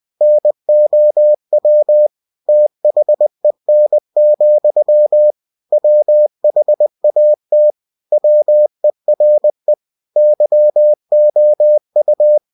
Morse code, heard as a keyed tone, about 20 wpm, 600 Hz. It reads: NOW THEN, WHAT WERE YOU
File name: cw-books / winnie-the-pooh / winnie-20wpm-page-0054